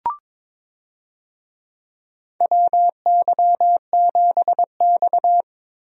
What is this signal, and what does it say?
Morse code, heard as a keyed tone, about 22 wpm, 700 Hz. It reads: WY7X
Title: USA Call Signs - 4 Characters - 22wpm